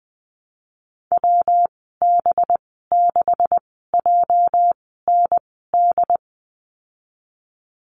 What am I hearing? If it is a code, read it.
Morse code, heard as a keyed tone, about 20 wpm, spaced as for 15 wpm, 700 Hz. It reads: WB6JND
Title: 213 Lesson 34g - Number 8 - Call Signs 20wpm